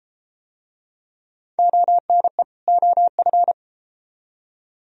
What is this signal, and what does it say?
Morse code, heard as a keyed tone, about 32 wpm, 700 Hz. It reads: ONE OF